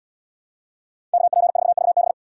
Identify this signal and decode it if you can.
Morse code, heard as a keyed tone, about 69 wpm, 700 Hz. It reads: KP4VB